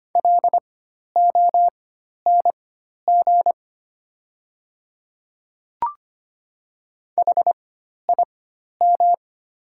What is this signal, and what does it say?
Morse code, heard as a keyed tone, about 25 wpm, spaced as for 12 wpm, 700 Hz. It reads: LONG HIM
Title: 4x Character Spacing - Top 100 Words - 25wpm